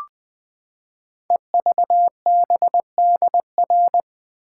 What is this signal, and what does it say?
Morse code, heard as a keyed tone, about 20 wpm, 700 Hz. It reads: EVBDR
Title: Code Groups - 5 Letter:Numbers - 20wpm